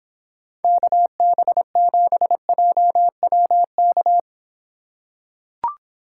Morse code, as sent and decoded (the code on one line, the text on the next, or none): -.- -... --... .--- .-- -.-
KB7JWK